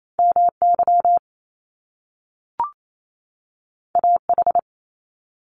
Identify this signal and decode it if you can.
Morse code, heard as a keyed tone, about 28 wpm, 700 Hz. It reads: MY AH
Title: Top Two Letter Combinations in Words 28wpm